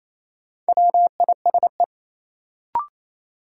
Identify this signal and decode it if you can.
Morse code, heard as a keyed tone, about 28 wpm, 700 Hz. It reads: WISE
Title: Sentences from Top 1600 Words - New Word Drill Alt - 28wpm